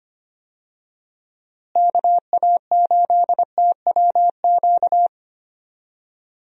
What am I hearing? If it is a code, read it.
Morse code, heard as a keyed tone, about 25 wpm, 700 Hz. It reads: KA8TWQ
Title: USA Call Signs - 6 Characters - Repeated 3x - 25wpm